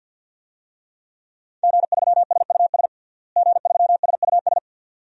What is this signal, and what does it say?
Morse code, heard as a keyed tone, about 52 wpm, 700 Hz. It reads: G3SUS G3SUS
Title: DXCC-100 Call Signs - 5 Characters - Repeated 2x - 50wpm